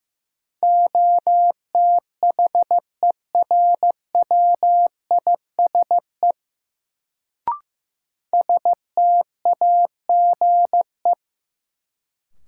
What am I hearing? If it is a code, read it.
Morse code, heard as a keyed tone, about 15 wpm, 700 Hz. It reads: OTHERWISE STAGE
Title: Sentences from Top 1200 Words - New Word Drill Alt - 15wpm